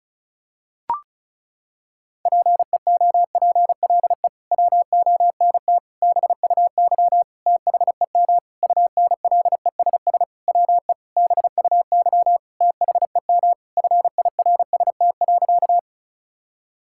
Morse code, heard as a keyed tone, about 36 wpm, 700 Hz. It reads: PEOPLE WONT BUY THEM UNLESS WE BUY THEM FIRST.
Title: Sentences from Top 1000 Words - Review Part 6 - 35wpm